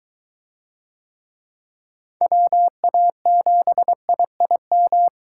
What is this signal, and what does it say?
Morse code, heard as a keyed tone, about 23 wpm, 700 Hz. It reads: WA7IIM